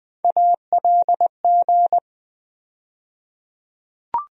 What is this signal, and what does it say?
Morse code, heard as a keyed tone, about 20 wpm, 700 Hz. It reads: ALG
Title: USA Call Sign Postfixes - 3 Letters 20wpm